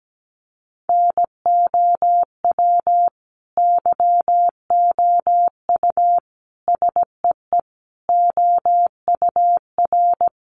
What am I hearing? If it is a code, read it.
Morse code, heard as a keyed tone, about 17 wpm, 700 Hz. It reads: NOW YOU SEE OUR
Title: Sentences from Top 1500 Words 17wpm